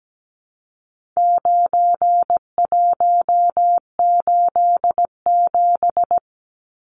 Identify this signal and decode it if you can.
Morse code, heard as a keyed tone, about 17 wpm, 700 Hz. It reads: 9187
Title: Numbers - 4-Digit 17wpm